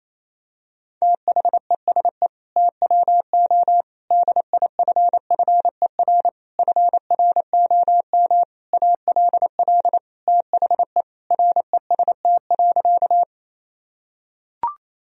Morse code, heard as a keyed tone, about 28 wpm, 700 Hz. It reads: THESE TWO DIFFER FROM ALL THE REST.
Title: Sentences from Top 1000 Words - Review Part 1 - 28wpm